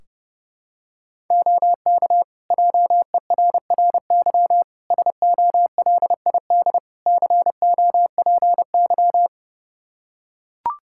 Morse code, heard as a keyed tone, about 30 wpm, 700 Hz. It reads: OK JERRY SOLID COPY
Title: Practice QSO - Exchange 3 - More Info - 30wpm